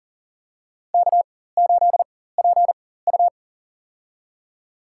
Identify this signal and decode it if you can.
Morse code, heard as a keyed tone, about 37 wpm, spaced as for 18 wpm, 700 Hz. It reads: K8PU
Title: USA Call Signs - 4x Character Spacing - 40wpm